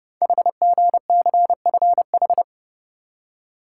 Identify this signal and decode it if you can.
Morse code, heard as a keyed tone, about 29 wpm, 690 Hz. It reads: HGCFH
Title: Code Groups - 5 Letters - 30wpm